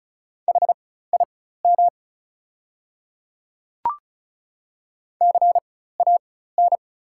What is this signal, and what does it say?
Morse code, heard as a keyed tone, about 36 wpm, spaced as for 16 wpm, 700 Hz. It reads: HIM CAN
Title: 4x Character Spacing - Top 100 Words - 35wpm